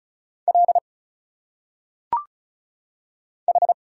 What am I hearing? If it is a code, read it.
Morse code, heard as a keyed tone, about 34 wpm, 700 Hz. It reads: L H
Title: Single Letters - Phonetic Pronunciation 35wpm